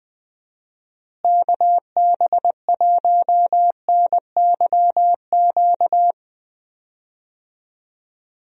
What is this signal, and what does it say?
Morse code, heard as a keyed tone, about 21 wpm, 700 Hz. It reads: KB1NYQ
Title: USA Call Signs - 6 Characters - Repeated 2x - 20wpm